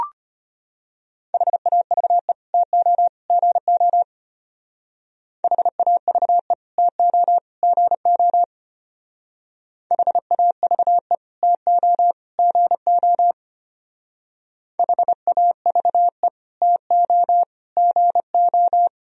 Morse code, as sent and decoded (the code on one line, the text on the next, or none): .... .- ...- . / - --- / --. --- / .... .- ...- . / - --- / --. --- / .... .- ...- . / - --- / --. --- / .... .- ...- . / - --- / --. ---
HAVE TO GO HAVE TO GO HAVE TO GO HAVE TO GO